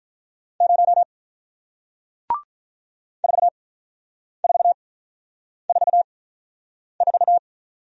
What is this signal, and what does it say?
Morse code, heard as a keyed tone, about 48 wpm, spaced as for 9 wpm, 700 Hz. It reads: 0 4444